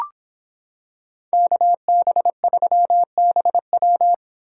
Morse code, heard as a keyed tone, about 26 wpm, 700 Hz. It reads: KB3BW